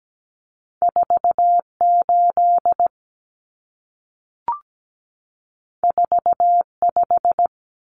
Morse code, heard as a keyed tone, about 17 wpm, 700 Hz. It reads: 48 45
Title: Common Cut Numbers - 2-Digit 17wpm